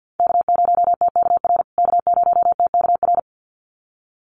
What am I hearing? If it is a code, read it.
Morse code, heard as a keyed tone, about 50 wpm, 700 Hz. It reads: K0TKR K0TKR
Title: USA Call Signs Repeated 2x 50wpm